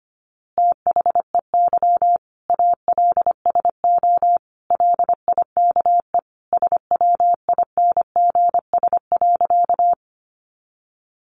Morse code, heard as a keyed tone, about 25 wpm, 700 Hz. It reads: THEY ALSO LIKE SWINGS.
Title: Easy Phrases 25wpm